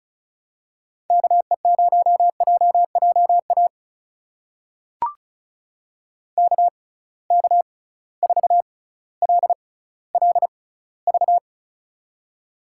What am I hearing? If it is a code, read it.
Morse code, heard as a keyed tone, about 36 wpm, spaced as for 18 wpm, 700 Hz. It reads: KE0JJA KK4LLV
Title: USA Call Signs - 6 Characters - 6x Character Spacing - 35wpm